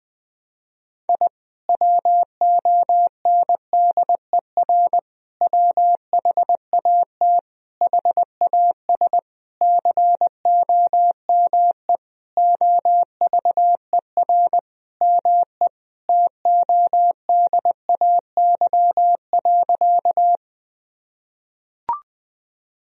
Morse code, heard as a keyed tone, about 21 wpm, 700 Hz. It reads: I WONDER WHAT HAS COME OVER ME TODAY.
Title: Sentences from Top 1000 Words - Review Part 4 - 20wpm